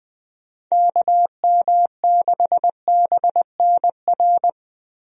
Morse code, heard as a keyed tone, about 21 wpm, 700 Hz. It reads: KM6BNR
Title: USA Call Signs - 6 Characters - Repeated 3x - 20wpm